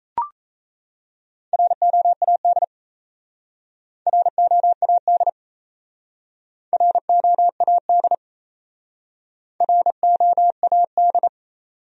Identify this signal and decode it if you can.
Morse code, heard as a keyed tone, about 36 wpm, 700 Hz. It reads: ROAD ROAD ROAD ROAD